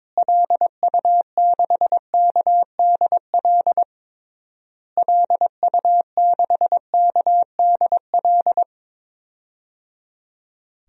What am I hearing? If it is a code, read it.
Morse code, heard as a keyed tone, about 22 wpm, 700 Hz. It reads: LU6KDL LU6KDL
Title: DXCC-100 Call Signs - 6 Characters - Repeated 3x - 22wpm